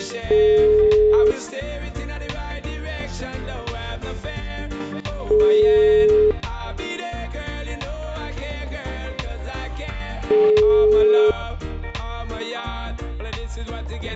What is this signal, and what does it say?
Ringback tone (Europe, 425 Hz).